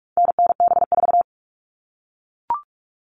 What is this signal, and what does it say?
Morse code, heard as a keyed tone, about 44 wpm, 700 Hz. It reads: NNB4
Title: All Cut Numbers - 4-Digit 45wpm